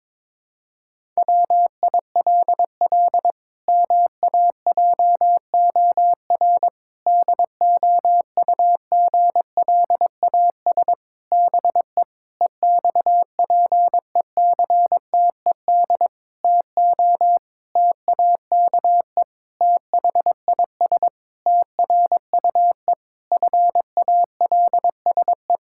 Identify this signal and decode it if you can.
Morse code, heard as a keyed tone, about 22 wpm, 700 Hz. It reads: WILL MAJOR DOUGLAS BE EXPECTED TO TAKE THIS TRUE FALSE
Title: Pangram Challenge v2 - 22wpm